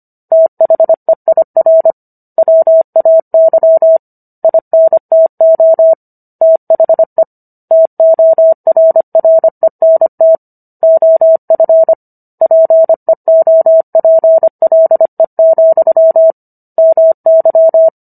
Morse code, heard as a keyed tone, about 25 wpm, 650 Hz. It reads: THEIR WAY INTO THE TORRENT OF PEOPLE, MY